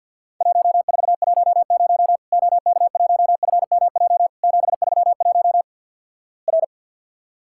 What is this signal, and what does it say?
Morse code, heard as a keyed tone, about 50 wpm, 700 Hz. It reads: 1410 OK1FMJ 731 R
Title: Worked All Europe DX Contest - QTC Exchange - 50wpm